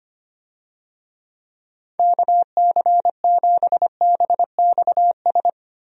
Morse code, heard as a keyed tone, about 25 wpm, 700 Hz. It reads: KC7BXS
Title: USA Call Signs - 6 Characters - Repeated 3x - 25wpm